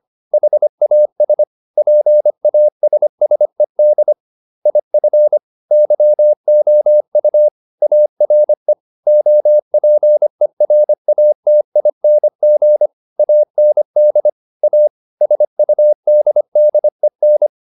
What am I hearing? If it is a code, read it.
Morse code, heard as a keyed tone, about 25 wpm, 600 Hz. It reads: HAS PASSED IF YOU ARE OPERATING AND A SUDDEN